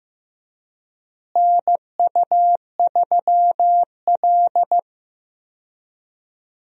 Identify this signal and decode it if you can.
Morse code, heard as a keyed tone, about 15 wpm, 700 Hz. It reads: NU3L
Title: USA Call Signs - 4 Characters - Repeated 3x - 15wpm